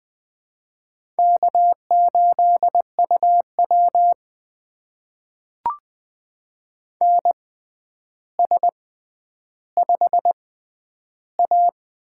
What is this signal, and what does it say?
Morse code, heard as a keyed tone, about 20 wpm, spaced as for 10 wpm, 700 Hz. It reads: K8UW NS5A
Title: USA Call Signs - 4 Characters - 6x Character Spacing - 20wpm